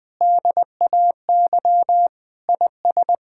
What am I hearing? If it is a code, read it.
Morse code, heard as a keyed tone, about 20 wpm, 700 Hz. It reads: DAY IS
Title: Difficult Sentences with Hints - 20wpm